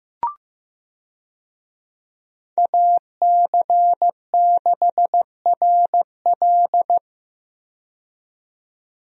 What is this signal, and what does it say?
Morse code, heard as a keyed tone, about 15 wpm, 700 Hz. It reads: AC6RL
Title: USA Call Signs - 5 Characters - 15wpm